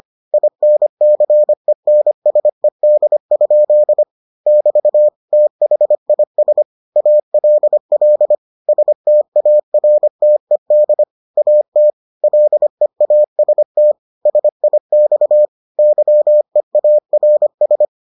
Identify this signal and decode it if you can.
Morse code, heard as a keyed tone, about 25 wpm, 600 Hz. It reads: INCENSED? <BT> THIS ALL STARTED AT LEAST SIX YEARS